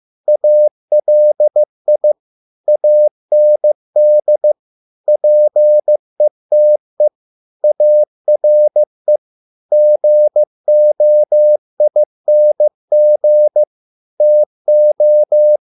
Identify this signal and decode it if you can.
Morse code, heard as a keyed tone, about 15 wpm, 600 Hz. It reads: ALI AND PETE ARE GOING TO